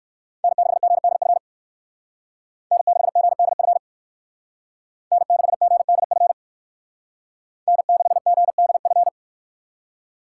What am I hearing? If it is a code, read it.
Morse code, heard as a keyed tone, about 53 wpm, 700 Hz. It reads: N6GDF N6GDF N6GDF N6GDF
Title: Speed Racing - USA 1x3 Call Signs 45wpm